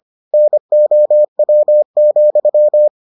Morse code, heard as a keyed tone, about 25 wpm, 600 Hz. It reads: NOW,